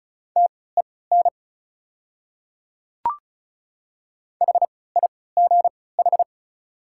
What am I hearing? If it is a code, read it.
Morse code, heard as a keyed tone, about 36 wpm, spaced as for 20 wpm, 690 Hz. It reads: TEN HIGH
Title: 3x Character Spacing - Top 500 Words - 35wpm